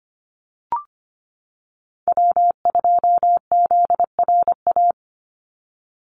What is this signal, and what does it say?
Morse code, heard as a keyed tone, about 25 wpm, 700 Hz. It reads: W2ZRA
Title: Call Signs - 25wpm